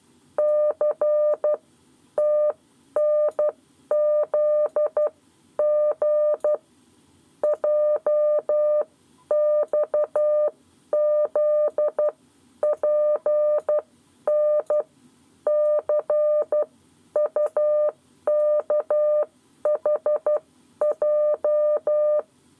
Morse code, heard as a keyed tone, about 12 wpm, spaced as for 10 wpm, 600 Hz. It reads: C TNZG JXZ PN C UKHJ
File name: 7 Lezione cw 1